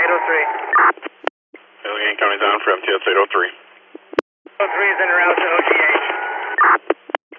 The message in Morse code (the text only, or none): none